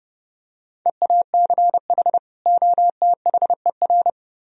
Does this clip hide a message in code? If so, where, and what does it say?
Yes: Morse code from 0.9 to 4.1 s, heard as a keyed tone, about 28 wpm, 700 Hz. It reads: EACH OTHER